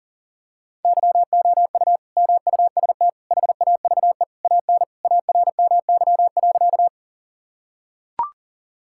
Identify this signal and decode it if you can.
Morse code, heard as a keyed tone, about 40 wpm, 700 Hz. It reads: YOU MUST HAVE AN ARMY.
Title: Sentences from Top 1000 Words - Review Part 5 - 40wpm